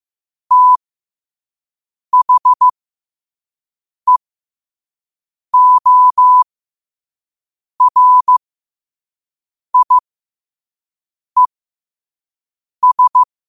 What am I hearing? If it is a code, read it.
Morse code, heard as a keyed tone, about 15 wpm, spaced as for 5 wpm, 1000 Hz. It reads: THEORIES
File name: 5wpm